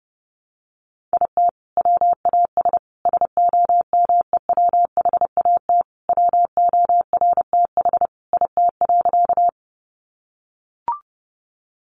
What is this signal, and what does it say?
Morse code, heard as a keyed tone, about 30 wpm, 700 Hz. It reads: IT WAS SOMEWHAT WORTH IT.
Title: Sentences from Top 1300 Words 30wpm